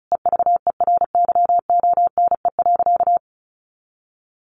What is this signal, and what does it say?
Morse code, heard as a keyed tone, about 34 wpm, 700 Hz. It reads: EVERYONE.
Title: A Sound Of Thunder 35wpm